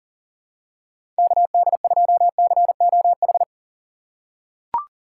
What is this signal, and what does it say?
Morse code, heard as a keyed tone, about 37 wpm, 700 Hz. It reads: KD2COH
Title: USA Call Signs - 2x Character Spacing - 40wpm